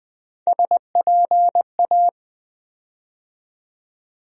Morse code, heard as a keyed tone, about 20 wpm, 700 Hz. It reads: SPA
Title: USA Call Sign Postfixes - 3 Letters 20wpm